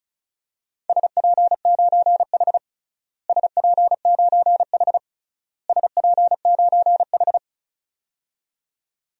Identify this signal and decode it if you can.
Morse code, heard as a keyed tone, about 36 wpm, 700 Hz. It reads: SP9H SP9H SP9H
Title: DXCC-100 Call Signs - 4 Characters - Repeated 3x - 35wpm